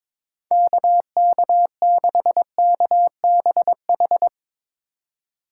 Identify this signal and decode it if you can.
Morse code, heard as a keyed tone, about 22 wpm, 700 Hz. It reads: KK6KBH
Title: Speed Racing - USA 2x3 Call Signs 22wpm